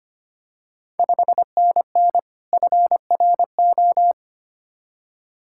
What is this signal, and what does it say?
Morse code, heard as a keyed tone, about 25 wpm, 700 Hz. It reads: 5NN FRO
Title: Ontario QSO Party Exchange 25wpm